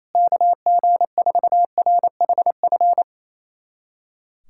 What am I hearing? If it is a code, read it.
Morse code, heard as a keyed tone, about 28 wpm, 700 Hz. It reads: KG4RHF
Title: USA Callsign DE Callsign - 6 Characters - 28wpm